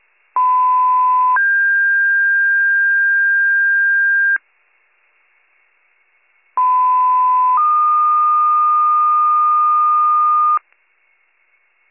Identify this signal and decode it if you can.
Two-tone page, heard as a one-second tone, then a three-second tone, about 1010 then 1600 Hz, about 1010 then 1220 Hz. It is heard from 0.4 to 10.6 s.